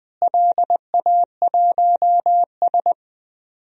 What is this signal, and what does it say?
Morse code, heard as a keyed tone, about 20 wpm, 700 Hz. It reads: LA1S
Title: DXCC-100 Call Signs - 4 Characters - Repeated 2x - 20wpm